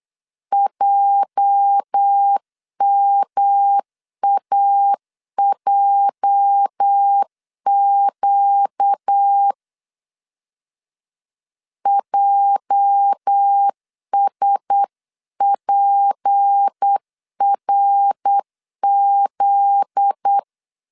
Morse code, heard as a keyed tone, about 8 wpm, 800 Hz. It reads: JMAJQ JSPRZ
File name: LEZIONI / 8 YJ45 / 8 YJ 8wpm 3313